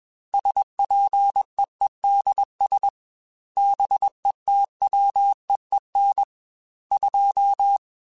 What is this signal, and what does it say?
Morse code, heard as a keyed tone, about 21 wpm, 800 Hz. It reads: SPEEDS BETWEEN 2
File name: Weather Sets / Weather Sets-mp3s / Weather 4 10 @20